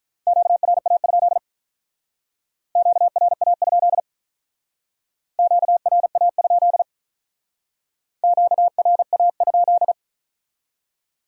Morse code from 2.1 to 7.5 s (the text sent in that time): QRA? QRA?